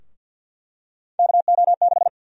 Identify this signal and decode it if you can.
Morse code, heard as a keyed tone, about 51 wpm, 700 Hz. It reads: KO6
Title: USA Call Sign Prefixes - Repeated 2x - 50wpm